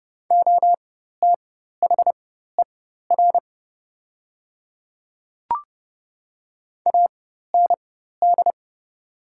Morse code, heard as a keyed tone, about 30 wpm, spaced as for 14 wpm, 700 Hz. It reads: OTHER AND
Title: 4x Character Spacing - Top 100 Words - 30wpm